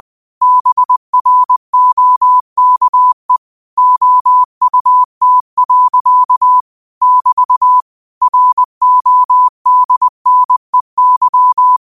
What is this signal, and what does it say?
Morse code, heard as a keyed tone, about 20 wpm, 1000 Hz. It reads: BROKE OUT. <BT> RODNEY